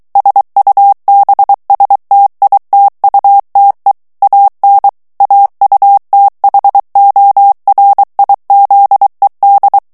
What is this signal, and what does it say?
Morse code, heard as a keyed tone, about 23 wpm, 800 Hz. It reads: SUBSTITUTE AN AUTHORIZED
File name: ANGLAIS texte clair 3 vitesse 22 mots minute